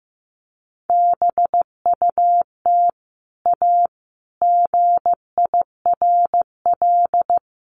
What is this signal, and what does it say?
Morse code, heard as a keyed tone, about 15 wpm, 700 Hz. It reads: BUT A GIRL
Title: Sentences from Top 1200 Words 15wpm